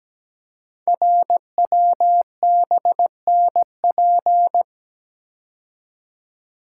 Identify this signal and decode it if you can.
Morse code, heard as a keyed tone, about 17 wpm, 700 Hz. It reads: RWBNP
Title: Code Groups - 5 Letters - 17wpm